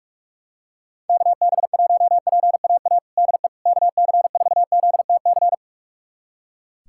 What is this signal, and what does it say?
Morse code, heard as a keyed tone, about 44 wpm, 700 Hz. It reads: KB1PAA DE KC4ZTC
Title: USA Callsign DE Callsign - 6 Characters - 45wpm